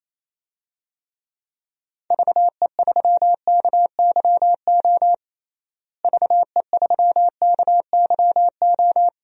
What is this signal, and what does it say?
Morse code, heard as a keyed tone, about 27 wpm, 700 Hz. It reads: VE3KYO VE3KYO
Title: DXCC-100 Call Signs - 6 Characters - Repeated 2x - 28wpm